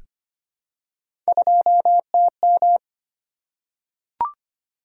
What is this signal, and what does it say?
Morse code, heard as a keyed tone, about 25 wpm, 700 Hz. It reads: 2TM
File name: Three Letter-Number Combinations - 25wpm